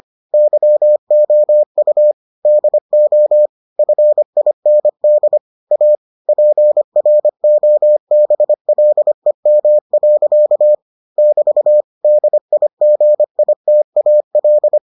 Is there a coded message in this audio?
Yes: Morse code from 0.3 to 14.8 s, heard as a keyed tone, about 25 wpm, 600 Hz. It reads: YOU DO FIND A PROBLEM. <BT> DIGITAL